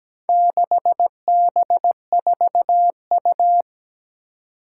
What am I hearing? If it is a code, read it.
Morse code, heard as a keyed tone, about 17 wpm, 700 Hz. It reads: 6B4U
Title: Four Challenging Letter-Number Combinations 17wpm